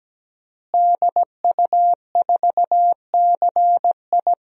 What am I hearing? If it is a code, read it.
Morse code, heard as a keyed tone, about 17 wpm, 700 Hz. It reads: DU4CI
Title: DXCC-100 Call Signs - 5 Characters - Repeated 2x - 17wpm